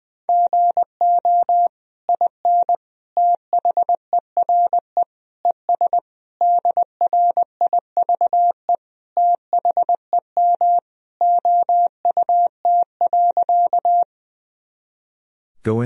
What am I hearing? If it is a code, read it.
Morse code, heard as a keyed tone, about 20 wpm, 700 Hz. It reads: GO IN THERE ES DRIVE THEM OUT.
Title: Sentences from Top 500 Words and Abbreviations 20wpm